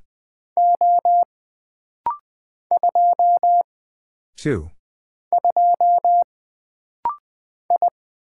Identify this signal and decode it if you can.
Morse code, heard as a keyed tone, about 20 wpm, 700 Hz. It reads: O 2 2 I
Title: Single Character (Letter:Number:Punctuation:Prosign) - Rapid-Fire - 20wpm